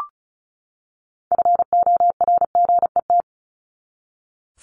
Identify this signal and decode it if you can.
Morse code, heard as a keyed tone, about 34 wpm, 700 Hz. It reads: FORGET
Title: Top 1001-1250 Words 35wpm